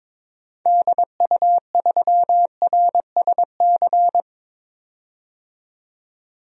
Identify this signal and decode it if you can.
Morse code, heard as a keyed tone, about 22 wpm, 700 Hz. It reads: DU3RSC
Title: DXCC-100 Call Signs - 6 Characters - 22wpm